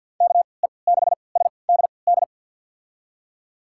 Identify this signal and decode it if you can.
Morse code, heard as a keyed tone, about 48 wpm, spaced as for 28 wpm, 700 Hz. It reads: KE6SDD
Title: USA Call Signs - 6 Characters - 3x Character Spacing - 50wpm